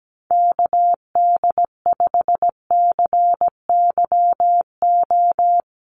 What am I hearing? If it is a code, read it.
Morse code, heard as a keyed tone, about 17 wpm, 700 Hz. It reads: KD5CYO
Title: USA Call Signs Repeated 3x 17wpm